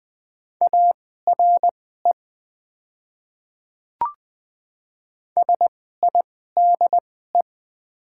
Morse code, heard as a keyed tone, about 20 wpm, spaced as for 14 wpm, 700 Hz. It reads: ARE SIDE
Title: 2x Character Spacing - Top 200 Words - 20wpm